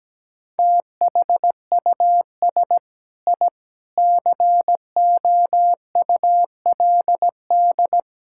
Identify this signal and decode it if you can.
Morse code, heard as a keyed tone, about 17 wpm, 700 Hz. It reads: THUS I COULD
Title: Sentences from Top 1000 Words - Review Part 5 - 17wpm